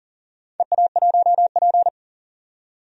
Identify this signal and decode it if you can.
Morse code, heard as a keyed tone, about 40 wpm, 700 Hz. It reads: EA1P
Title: DXCC-100 Call Signs - 4 Characters - 40wpm